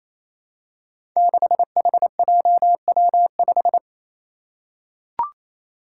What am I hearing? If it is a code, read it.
Morse code, heard as a keyed tone, about 27 wpm, 700 Hz. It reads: 6HJW5
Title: Five Challenging Letter-Number Combinations 28wpm